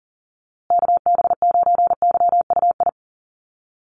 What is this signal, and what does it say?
Morse code, heard as a keyed tone, about 40 wpm, 700 Hz. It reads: KB9YUI
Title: USA Call Signs Repeated 3x 40wpm